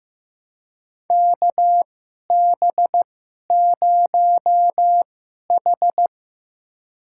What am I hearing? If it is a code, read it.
Morse code, heard as a keyed tone, about 15 wpm, spaced as for 11 wpm, 700 Hz. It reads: KB0H
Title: USA Call Signs - 4 Characters - 2x Character Spacing - 15wpm